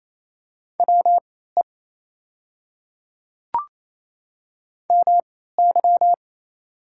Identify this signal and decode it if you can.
Morse code, heard as a keyed tone, about 28 wpm, spaced as for 16 wpm, 700 Hz. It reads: WE MY
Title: 3x Character Spacing - Top 100 Words - 28wpm